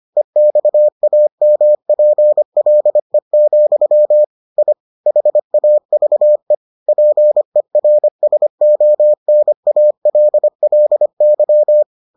Morse code, heard as a keyed tone, about 25 wpm, 600 Hz. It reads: EXAMPLE, I HAVE PERSONALLY